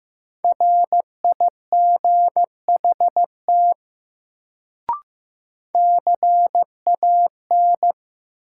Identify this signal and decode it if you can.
Morse code, heard as a keyed tone, about 15 wpm, 700 Hz. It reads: RIGHT CAN
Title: Sets of 2 Words - New Word Drill - 15wpm